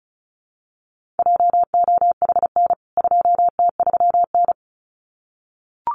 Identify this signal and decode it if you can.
Morse code, heard as a keyed tone, about 34 wpm, 700 Hz. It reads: JOHN 2T3N
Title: CWT Exchange 35wpm